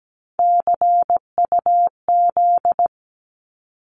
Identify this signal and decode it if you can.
Morse code, heard as a keyed tone, about 17 wpm, 700 Hz. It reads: CUZ
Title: Abbreviations 17wpm